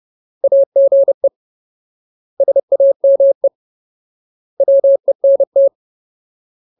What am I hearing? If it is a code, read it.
Morse code, heard as a keyed tone, about 30 wpm, 550 Hz. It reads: AGE SAME WENT
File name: Ultimate Word List / 30WPM Ultimate